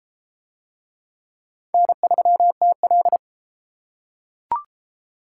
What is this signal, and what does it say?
Morse code, heard as a keyed tone, about 31 wpm, 700 Hz. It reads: N3TL